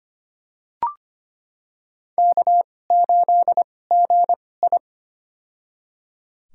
Morse code, heard as a keyed tone, about 25 wpm, spaced as for 18 wpm, 700 Hz. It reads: K8GI
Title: USA Call Signs - 4 Characters - 2x Character Spacing - 25wpm